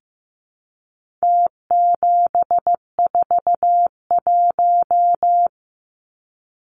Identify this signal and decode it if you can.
Morse code, heard as a keyed tone, about 15 wpm, 700 Hz. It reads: T741
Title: Numbers - Contest Style - 4-Digit 15wpm